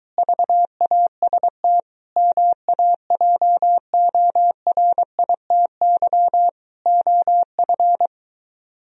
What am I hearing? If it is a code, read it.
Morse code, heard as a keyed tone, about 22 wpm, 700 Hz. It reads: VAST MAJORITY OF